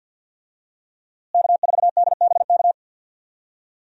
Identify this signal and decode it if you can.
Morse code, heard as a keyed tone, about 52 wpm, 700 Hz. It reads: K4DBK
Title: USA Call Signs - 5 Characters - Repeated 2x - 50wpm